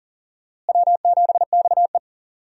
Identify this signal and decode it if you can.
Morse code, heard as a keyed tone, about 40 wpm, 700 Hz. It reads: W7XE
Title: USA Call Signs - 4 Characters - Repeated 3x - 40wpm